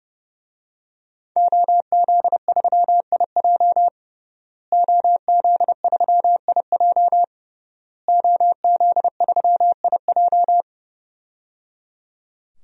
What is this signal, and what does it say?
Morse code, heard as a keyed tone, about 30 wpm, 700 Hz. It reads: OZ3IJ OZ3IJ OZ3IJ
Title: DXCC-100 Call Signs - 5 Characters - Repeated 3x - 30wpm